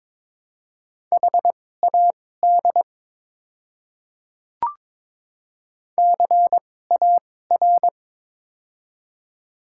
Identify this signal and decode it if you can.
Morse code, heard as a keyed tone, about 22 wpm, spaced as for 16 wpm, 700 Hz. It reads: HAD CAR
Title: 2x Character Spacing - Top 300 Words - 22wpm